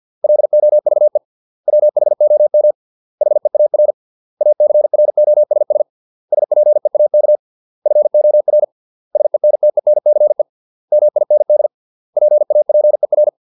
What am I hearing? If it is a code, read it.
Morse code, heard as a keyed tone, about 50 wpm, 600 Hz. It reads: LOVE WHOM HEAR ACROSS SPEAK FOR SENTENCE MIND PAPER